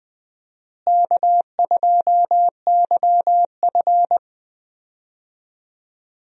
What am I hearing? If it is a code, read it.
Morse code, heard as a keyed tone, about 20 wpm, 700 Hz. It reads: K2YF
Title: USA Call Signs - 4 Characters - Repeated 2x - 20wpm